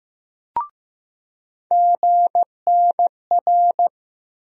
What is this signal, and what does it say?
Morse code, heard as a keyed tone, about 16 wpm, 700 Hz. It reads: GNR
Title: USA Call Sign Postfixes - 3 Letters 15wpm